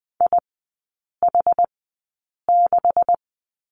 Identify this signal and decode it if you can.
Morse code, heard as a keyed tone, about 20 wpm, spaced as for 8 wpm, 690 Hz. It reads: IH6
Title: 133 Lesson 22e - Question Mark - Three Characters 20wpm